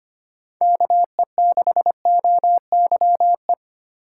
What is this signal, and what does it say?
Morse code, heard as a keyed tone, about 25 wpm, 700 Hz. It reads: KE6OYE
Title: USA Call Signs - 6 Characters - 2x Character Spacing - 25wpm